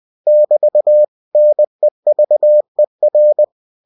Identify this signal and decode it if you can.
Morse code, heard as a keyed tone, about 20 wpm, 600 Hz. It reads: <BT> NEVER
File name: cw-books / winnie-the-pooh / winnie-20wpm-page-0061